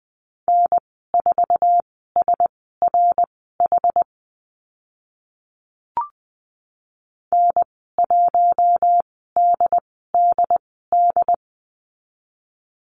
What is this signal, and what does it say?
Morse code, heard as a keyed tone, about 20 wpm, spaced as for 14 wpm, 700 Hz. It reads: N4SRH N1DDD
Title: 076 Lesson 13g - Letter L - Call Signs 20wpm